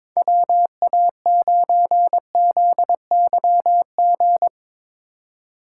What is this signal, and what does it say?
Morse code, heard as a keyed tone, about 21 wpm, 700 Hz. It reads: WA9ZYG
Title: USA Call Signs - 8x Character Spacing - 22wpm